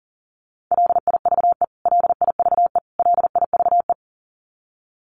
Morse code, heard as a keyed tone, about 40 wpm, 700 Hz. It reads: LIVE LIVE LIVE
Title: Top 200 Words Repeated 40wpm